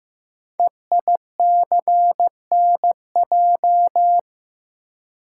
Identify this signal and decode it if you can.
Morse code, heard as a keyed tone, about 15 wpm, 700 Hz. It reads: EICNJ
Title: Code Groups - 5 Letters - 15wpm